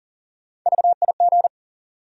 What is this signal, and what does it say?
Morse code, heard as a keyed tone, about 40 wpm, 700 Hz. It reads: VIG
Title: USA Call Sign Postfixes - 3 Letters 40wpm